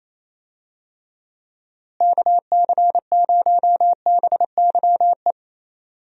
Morse code, heard as a keyed tone, about 28 wpm, 700 Hz. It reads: KC0BYE
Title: USA Call Signs - 6 Characters - Repeated 2x - 28wpm